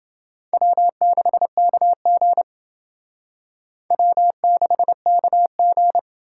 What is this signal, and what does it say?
Morse code, heard as a keyed tone, about 28 wpm, 700 Hz. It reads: W6KG W6KG